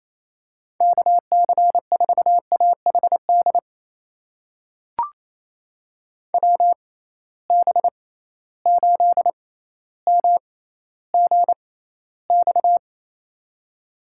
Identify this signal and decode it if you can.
Morse code, heard as a keyed tone, about 28 wpm, spaced as for 14 wpm, 700 Hz. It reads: KC4AHD WB8MGX
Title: USA Call Signs - 6 Characters - 6x Character Spacing - 28wpm